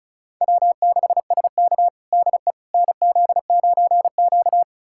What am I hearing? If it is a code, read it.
Morse code, heard as a keyed tone, about 36 wpm, 700 Hz. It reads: W6SK DE NZ9Q
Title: USA Callsign DE Callsign - 4 Characters - 35wpm